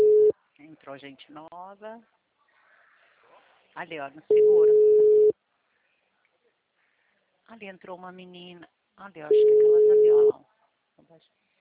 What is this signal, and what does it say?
Ringback tone (Europe, 425 Hz), with speech.